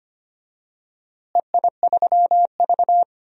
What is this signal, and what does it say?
Morse code, heard as a keyed tone, about 25 wpm, 700 Hz. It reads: EI3V
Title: DXCC-100 Call Signs - 4 Characters - Repeated 3x - 25wpm